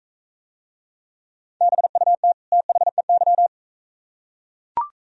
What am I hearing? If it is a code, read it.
Morse code, heard as a keyed tone, about 42 wpm, 700 Hz. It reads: BUT THEY